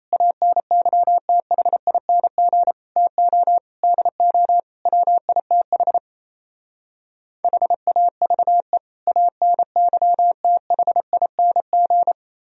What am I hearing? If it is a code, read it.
Morse code, heard as a keyed tone, about 35 wpm, 700 Hz. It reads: ANYTHING TO DO WITH HAVE ANYTHING
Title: Speed-Racing - Sets of 5 Words 28wpm